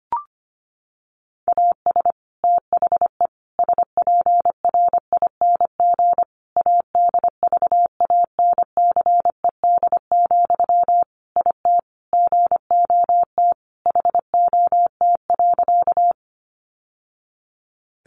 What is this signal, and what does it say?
Morse code, heard as a keyed tone, about 25 wpm, 700 Hz. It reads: AS THE SPRING ADVANCED, IT GOT HOT.
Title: Sentences from Top 1200 Words 25wpm